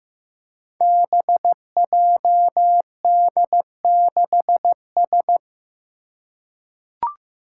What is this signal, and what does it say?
Morse code, heard as a keyed tone, about 15 wpm, 700 Hz. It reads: BJD6S
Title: Five Challenging Letter-Number Combinations 15wpm